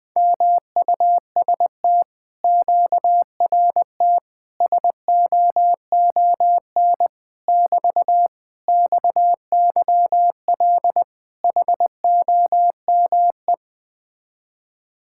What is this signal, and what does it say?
Morse code, heard as a keyed tone, about 20 wpm, 700 Hz. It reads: MUST QRT SOON <BT> XYL HOME
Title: Speed-Racing Common QSO Phrases - 17wpm